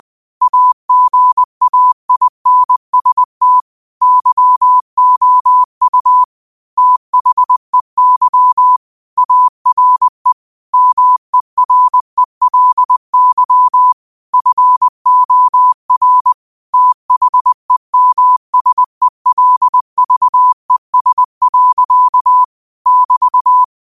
Morse code, heard as a keyed tone, about 20 wpm, 1000 Hz. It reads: AGAINST YOU THEY ARE MERELY FOR THEMSELVES. <BT>
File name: qotdfiles / 20wpm